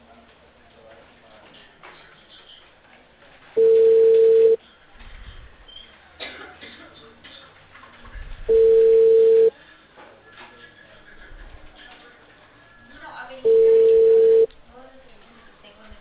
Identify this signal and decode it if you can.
Ringback tone (China, 450 Hz), with speech.